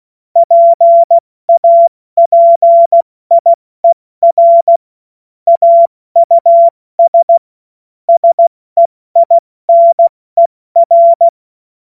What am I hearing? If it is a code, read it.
Morse code, heard as a keyed tone, about 16 wpm, 680 Hz. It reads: PAPIER AUS SEINER
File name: Morsekurs / Witze / Witz 23. Dezember 2025 70 F80